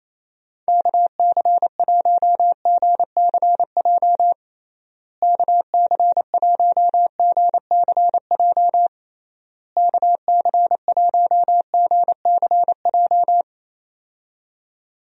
Morse code, heard as a keyed tone, about 28 wpm, 700 Hz. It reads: KC1GCJ KC1GCJ KC1GCJ
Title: USA Call Signs - 6 Characters - Repeated 3x - 28wpm